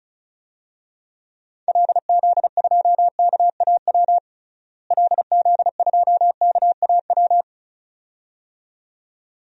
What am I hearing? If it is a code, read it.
Morse code, heard as a keyed tone, about 36 wpm, 700 Hz. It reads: LZ2KAW LZ2KAW
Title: DXCC-100 Call Signs - 6 Characters - Repeated 2x - 35wpm